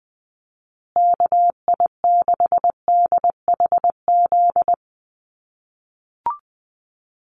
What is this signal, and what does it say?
Morse code, heard as a keyed tone, about 20 wpm, 700 Hz. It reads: KI6DHZ
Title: USA Call Signs Repeated 2x 20wpm